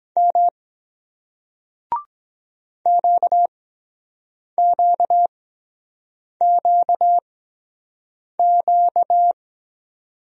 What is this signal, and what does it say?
Morse code, heard as a keyed tone, about 23 wpm, spaced as for 7 wpm, 700 Hz. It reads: M QQQQ